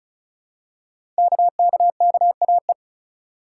Morse code, heard as a keyed tone, about 36 wpm, 700 Hz. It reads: KKKAE
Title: Code Groups - 5 Letters - 35wpm